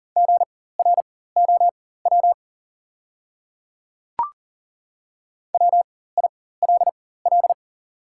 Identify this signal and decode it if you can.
Morse code, heard as a keyed tone, about 37 wpm, spaced as for 18 wpm, 700 Hz. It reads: GROW WILL
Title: 4x Character Spacing - Top 300 Words - 40wpm